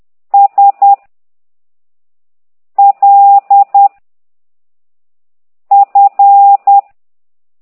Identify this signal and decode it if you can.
Morse code, heard as a keyed tone, about 10 wpm, 800 Hz. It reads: S L F
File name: ct2idw / exerc13